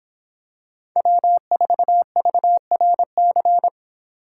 Morse code, heard as a keyed tone, about 24 wpm, 700 Hz. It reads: W4VRC